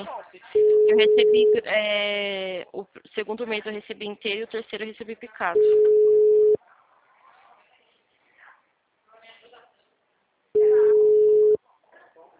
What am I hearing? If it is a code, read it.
Ringback tone (Europe, 425 Hz), with speech.
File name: out- -2011- - - .49093